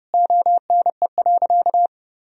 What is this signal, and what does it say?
Morse code, heard as a keyed tone, about 30 wpm, 700 Hz. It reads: ONE.
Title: Sentences from Top 400 Words and Abbreviations 30wpm